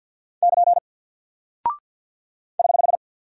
Morse code, heard as a keyed tone, about 48 wpm, 700 Hz. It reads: <KN> <HH>
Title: Single Character (Letter:Number:Punctuation:Prosign) Alt 50wpm